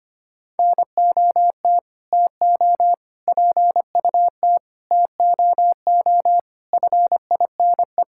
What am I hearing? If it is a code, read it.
Morse code, heard as a keyed tone, about 25 wpm, 700 Hz. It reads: NOT TO PUT TOO FINE
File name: Sets of 9 Words - 25wpm